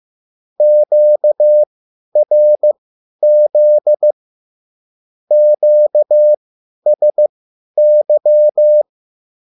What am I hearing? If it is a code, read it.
Morse code, heard as a keyed tone, about 15 wpm, spaced as for 10 wpm, 600 Hz. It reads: QRZ QSY
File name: QSL / CW lesson